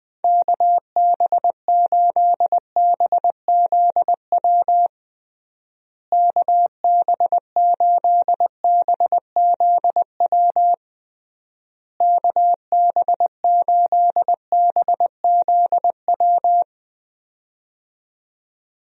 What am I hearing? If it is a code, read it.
Morse code, heard as a keyed tone, about 20 wpm, 700 Hz. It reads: KB8BZW KB8BZW KB8BZW
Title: USA Call Signs - 6 Characters - Repeated 3x - 20wpm